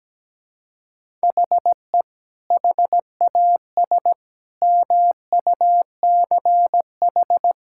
Morse code, heard as a keyed tone, about 17 wpm, 700 Hz. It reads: HE HAS MUCH
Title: Sentences from Top 1000 Words - Review Part 5 - 17wpm